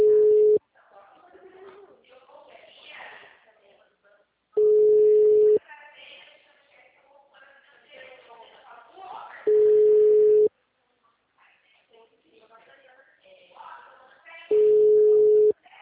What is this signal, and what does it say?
Ringback tone (Europe, 425 Hz), with speech.